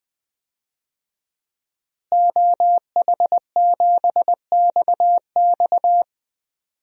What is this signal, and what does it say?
Morse code, heard as a keyed tone, about 20 wpm, 700 Hz. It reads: OH7XX
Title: DXCC-100 Call Signs - 5 Characters - Repeated 2x - 20wpm